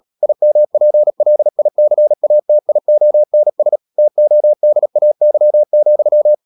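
Morse code, heard as a keyed tone, about 37 wpm, 600 Hz. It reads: IMPLICATIONS TODAY,
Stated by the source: Morse code practice at 35–39 wpm sending English text of Ken Follett's novel Never, from CW med Gnister.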